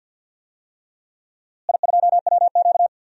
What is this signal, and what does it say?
Morse code, heard as a keyed tone, about 46 wpm, 700 Hz. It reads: I2WQ
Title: DXCC-100 Call Signs - 4 Characters - Repeated 3x - 50wpm